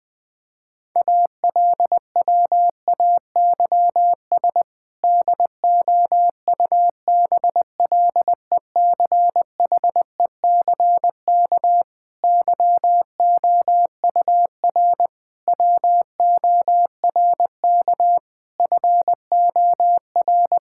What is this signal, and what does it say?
Morse code, heard as a keyed tone, about 20 wpm, 700 Hz. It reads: ALWAYS DOUBLECHECK YOUR WORK FOR
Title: Difficult Sentences with Hints - 20wpm